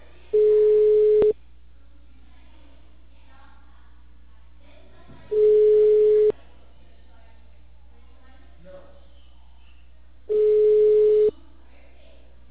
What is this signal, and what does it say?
Ringback tone (Europe, 425 Hz), with speech.